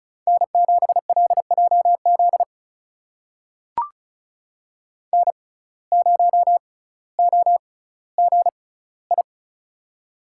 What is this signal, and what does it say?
Morse code, heard as a keyed tone, about 34 wpm, spaced as for 18 wpm, 700 Hz. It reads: N7LJZ N0OGI
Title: USA Call Signs - 5 Characters - 6x Character Spacing - 35wpm